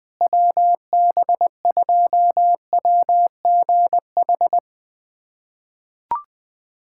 Morse code, heard as a keyed tone, about 20 wpm, 700 Hz. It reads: WB2WGH
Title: USA Callsign DE Callsign - 6 Characters - 20wpm